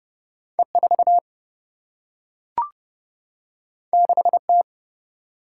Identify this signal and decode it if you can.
Morse code, heard as a keyed tone, about 31 wpm, 700 Hz. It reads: E4 6T
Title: Single-Letter Single-Number 30wpm